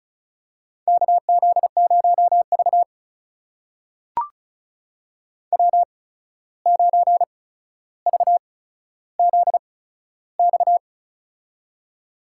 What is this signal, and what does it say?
Morse code, heard as a keyed tone, about 36 wpm, spaced as for 14 wpm, 700 Hz. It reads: KZ0V W9VZX
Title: USA Call Signs - 8x Character Spacing - 35wpm